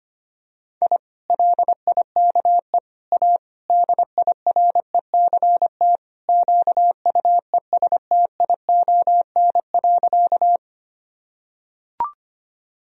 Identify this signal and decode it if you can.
Morse code, heard as a keyed tone, about 25 wpm, 700 Hz. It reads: I LIKE A DIRECT QUESTION.
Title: Sentences from Top 1000 Words - Review Part 2 - 25wpm